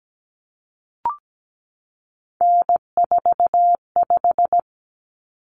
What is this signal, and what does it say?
Morse code, heard as a keyed tone, about 17 wpm, 700 Hz. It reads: N45
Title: Common Cut Numbers - 3-Digit 17wpm